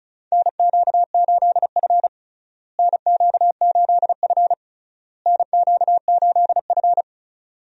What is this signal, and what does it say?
Morse code, heard as a keyed tone, about 34 wpm, 700 Hz. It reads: NQ8F NQ8F NQ8F
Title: USA Call Signs - 4 Characters - Repeated 3x - 35wpm